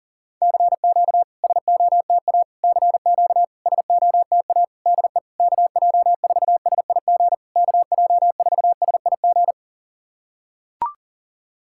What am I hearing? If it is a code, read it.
Morse code, heard as a keyed tone, about 40 wpm, 700 Hz. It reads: CQ SOTA CQ SOTA DE KJ4SIG KJ4SIG
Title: SOTA QSO Practice 40wpm